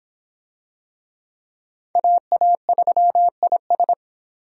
Morse code, heard as a keyed tone, about 26 wpm, 700 Hz. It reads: AA3IS